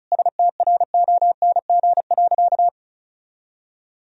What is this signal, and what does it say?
Morse code, heard as a keyed tone, about 36 wpm, 700 Hz. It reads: STRONG.
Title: Sentences from Top 1000 Words - Review Part 4 - 35wpm